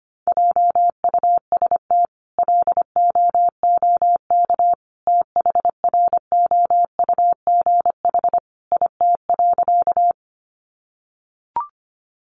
Morse code, heard as a keyed tone, about 25 wpm, 700 Hz. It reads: JUST LOOK THROUGH IT.
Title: Sentences from Top 200 Words 25wpm